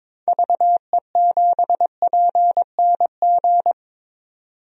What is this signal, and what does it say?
Morse code, heard as a keyed tone, about 22 wpm, 700 Hz. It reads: VE7PNG
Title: DXCC-100 Call Signs - 6 Characters - 22wpm